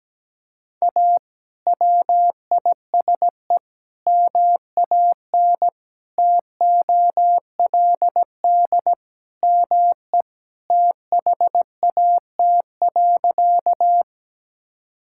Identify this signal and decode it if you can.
Morse code, heard as a keyed tone, about 17 wpm, 700 Hz. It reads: A WISE MAN TOLD ME THAT.
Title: Sentences from Top 1600 Words 17wpm